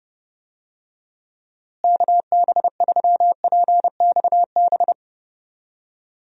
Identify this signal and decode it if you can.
Morse code, heard as a keyed tone, about 30 wpm, 700 Hz. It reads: KB3PXB
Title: USA Call Signs - 6 Characters - Repeated 2x - 30wpm